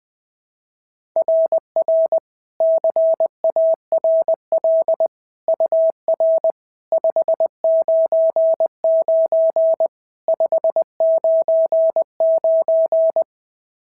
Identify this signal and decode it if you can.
Morse code, heard as a keyed tone, about 20 wpm, 650 Hz. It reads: RR CARL UR 599 599
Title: SOTA QSO Practice 20wpm